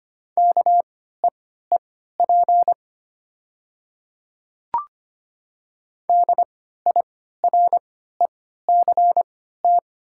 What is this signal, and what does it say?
Morse code, heard as a keyed tone, about 25 wpm, spaced as for 14 wpm, 700 Hz. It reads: KEEP DIRECT
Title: 3x Character Spacing - Top 500 Words - 25wpm